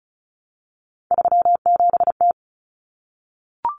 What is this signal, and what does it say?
Morse code, heard as a keyed tone, about 34 wpm, 700 Hz. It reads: 37T
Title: Common Cut Numbers - 3-Digit 35wpm